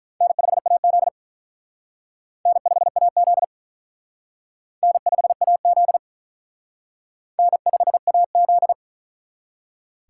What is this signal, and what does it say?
Morse code, heard as a keyed tone, about 44 wpm, 700 Hz. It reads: N5AZ N5AZ N5AZ N5AZ